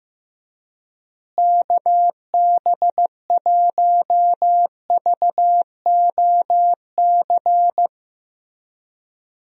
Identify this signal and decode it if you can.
Morse code, heard as a keyed tone, about 15 wpm, 700 Hz. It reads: KB1VOC
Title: USA Call Signs - 6 Characters - Repeated 2x - 15wpm